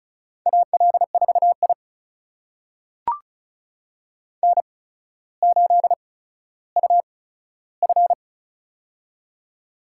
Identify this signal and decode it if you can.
Morse code, heard as a keyed tone, about 36 wpm, spaced as for 15 wpm, 700 Hz. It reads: AL4I N8UF
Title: USA Call Signs - 4 Characters - 8x Character Spacing - 35wpm